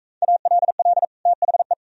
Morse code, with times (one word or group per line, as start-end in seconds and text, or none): 0.2-1.0 ALL
1.3-1.7 THE